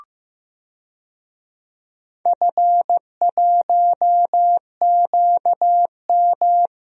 Morse code, heard as a keyed tone, about 15 wpm, 700 Hz. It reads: F1QM
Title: DXCC-100 Call Signs - 4 Characters - 15wpm